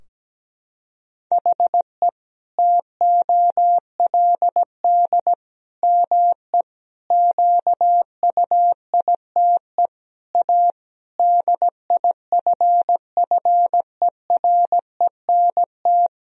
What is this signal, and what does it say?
Morse code, heard as a keyed tone, about 17 wpm, 700 Hz. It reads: HE TOLD ME QUITE A DIFFERENT
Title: Sentences from Top 1000 Words - Review Part 3 - 17wpm